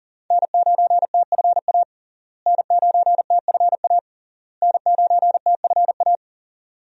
Morse code, heard as a keyed tone, about 40 wpm, 700 Hz. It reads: N9TFA N9TFA N9TFA
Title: USA Call Signs - 5 Characters - Repeated 3x - 40wpm